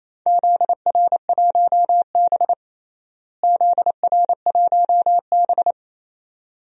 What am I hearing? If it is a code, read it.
Morse code, heard as a keyed tone, about 28 wpm, 700 Hz. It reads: ZR1B ZR1B
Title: DXCC-100 Call Signs - 4 Characters - Repeated 2x - 28wpm